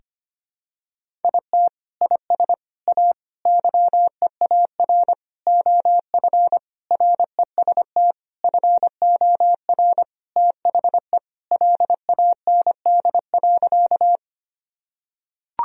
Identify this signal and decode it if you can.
Morse code, heard as a keyed tone, about 25 wpm, 700 Hz. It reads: IT IS A YEAR OF REST FOR THE LAND.
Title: Sentences from Top 1000 Words - Review Part 1 - 25wpm